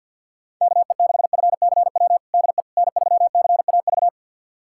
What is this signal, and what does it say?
Morse code, heard as a keyed tone, about 46 wpm, 700 Hz. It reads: KE6FCW DE N3CAV
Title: USA Callsign DE Callsign 50wpm